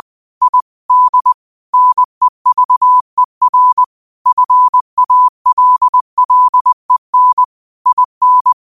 Morse code, heard as a keyed tone, about 20 wpm, 1000 Hz. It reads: ID NEVER FALLEN IN
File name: qotdfiles / 20wpm